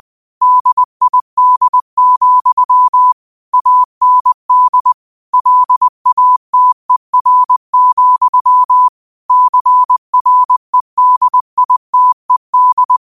Morse code, heard as a keyed tone, about 20 wpm, 1000 Hz. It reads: DID, AND LATER, CREDITED